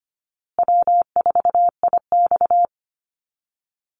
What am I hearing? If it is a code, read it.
Morse code, heard as a keyed tone, about 25 wpm, 700 Hz. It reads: W4IX
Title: Easy Call Signs Repeated 3x - 25wpm